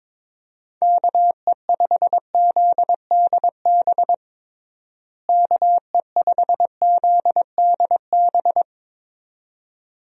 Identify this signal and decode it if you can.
Morse code, heard as a keyed tone, about 22 wpm, 700 Hz. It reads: KE5ZDB KE5ZDB
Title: USA Call Signs - 6 Characters - Repeated 2x - 22wpm